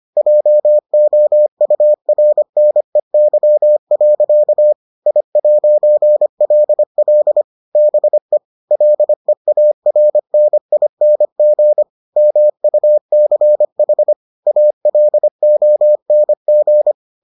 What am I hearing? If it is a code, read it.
Morse code, heard as a keyed tone, about 25 wpm, 600 Hz. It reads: JOURNEY. I'LL BE LEARNING MUCH ALONG